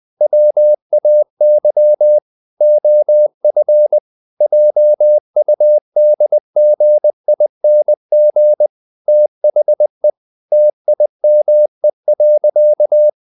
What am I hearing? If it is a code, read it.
Morse code, heard as a keyed tone, about 20 wpm, 600 Hz. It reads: WAY OF JUDGING THE TIME.
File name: the old man and the sea part 2 20wpm